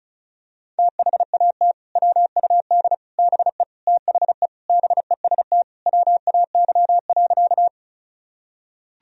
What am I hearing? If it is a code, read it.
Morse code, heard as a keyed tone, about 34 wpm, 700 Hz. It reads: THAT WUD BE THE BEST WAY.
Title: Sentences from Top 400 Words and Abbreviations 35wpm